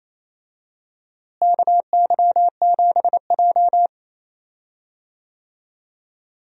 Morse code, heard as a keyed tone, about 28 wpm, 700 Hz. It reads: KY7J
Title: USA Call Signs - 4 Characters - Repeated 3x - 28wpm